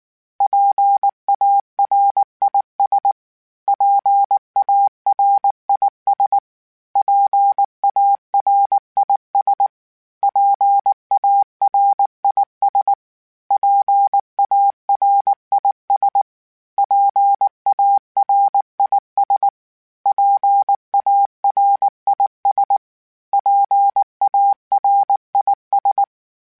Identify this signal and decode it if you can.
Morse code, heard as a keyed tone, about 19 wpm, 800 Hz. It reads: PARIS PARIS PARIS PARIS PARIS PARIS PARIS PARIS
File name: PARIS Calibration / PARIS mp3s / PARIS@18